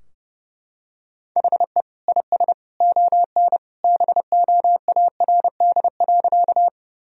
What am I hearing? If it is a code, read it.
Morse code, heard as a keyed tone, about 32 wpm, 700 Hz. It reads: HE IS ON BOARD.
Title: Sentences from Top 1000 Words - Review Part 4 - 30wpm